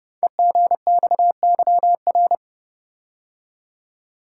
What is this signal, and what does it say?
Morse code, heard as a keyed tone, about 30 wpm, 690 Hz. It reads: EGXYR
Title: Code Groups - 5 Letters - 30wpm